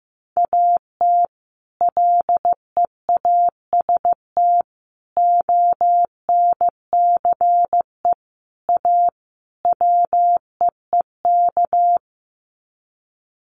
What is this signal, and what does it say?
Morse code, heard as a keyed tone, about 15 wpm, 700 Hz. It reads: AT LEAST ONCE A WEEK
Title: Sets of 5 Words Encore2 15wpm